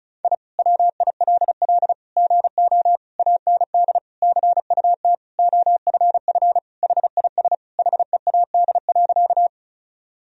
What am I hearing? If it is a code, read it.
Morse code, heard as a keyed tone, about 34 wpm, 700 Hz. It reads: I WILL GO AND CUT OFF HIS HEAD.
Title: Sentences from Top 1000 Words - Review Part 1 - 35wpm